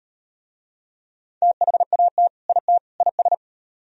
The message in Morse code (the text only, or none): THAT IT IS